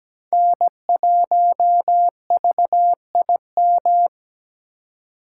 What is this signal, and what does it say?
Morse code, heard as a keyed tone, about 17 wpm, 700 Hz. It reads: N1VIM
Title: USA Call Signs - 5 Characters - 4x Character Spacing - 17wpm